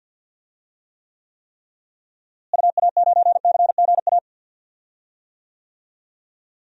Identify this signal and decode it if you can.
Morse code, heard as a keyed tone, about 46 wpm, 700 Hz. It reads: UA9CGA
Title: DXCC-100 Call Signs - 6 Characters - Repeated 3x - 50wpm